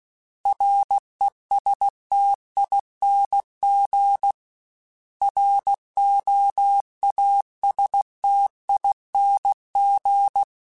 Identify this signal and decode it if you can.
Morse code, heard as a keyed tone, about 16 wpm, 800 Hz. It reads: RESTING ROASTING
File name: Special INGs / Special INGs-mp3s / Sngl ing @15